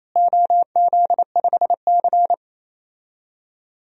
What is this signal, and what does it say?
Morse code, heard as a keyed tone, about 28 wpm, 700 Hz. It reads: OZ5C
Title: DXCC-100 Call Signs - 4 Characters - Repeated 2x - 28wpm